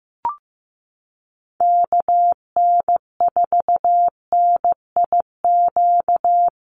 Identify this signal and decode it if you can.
Morse code, heard as a keyed tone, about 15 wpm, 700 Hz. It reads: KN4NIQ
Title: USA Call Signs Repeated 3x 15wpm